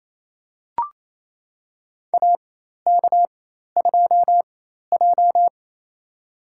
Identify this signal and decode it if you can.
Morse code, heard as a keyed tone, about 28 wpm, spaced as for 13 wpm, 700 Hz. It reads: AK2J
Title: USA Call Signs - 4 Characters - 4x Character Spacing - 28wpm